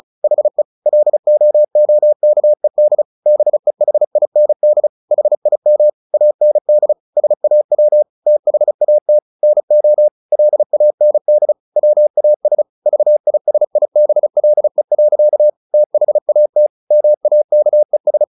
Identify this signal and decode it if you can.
Morse code, heard as a keyed tone, about 34 wpm, 600 Hz. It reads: HE LOOKED BEHIND HIM AND SAW THAT NO LAND WAS VISIBLE. THAT MAKES
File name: the old man and the sea / the old man and the sea 1 35wpm